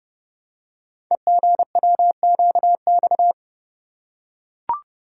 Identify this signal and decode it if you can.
Morse code, heard as a keyed tone, about 30 wpm, 700 Hz. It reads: EGWQX
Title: Code Groups - 5 Letters - 30wpm